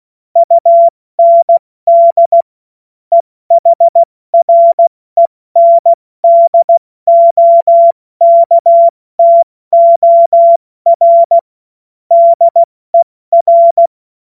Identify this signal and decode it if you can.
Morse code, heard as a keyed tone, about 16 wpm, 680 Hz. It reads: UND EHRENDOKTOR DER
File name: Morsekurs / Witze / Witz 09. August 2022 70 F80